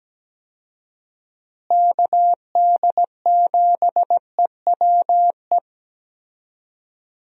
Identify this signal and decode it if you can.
Morse code, heard as a keyed tone, about 17 wpm, 700 Hz. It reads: KD7EWE
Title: USA Call Signs - 6 Characters - Repeated 3x - 17wpm